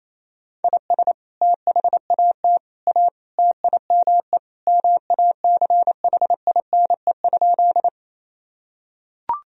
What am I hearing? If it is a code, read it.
Morse code, heard as a keyed tone, about 28 wpm, 700 Hz. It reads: IS THAT A TIME MACHINE?
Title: Sentences from Top 1000 Words - Review Part 2 - 28wpm